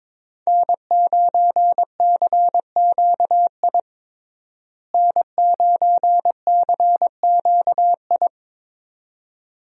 Morse code, heard as a keyed tone, about 22 wpm, 700 Hz. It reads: N9CQI N9CQI
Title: USA Call Signs - 5 Characters - Repeated 2x - 22wpm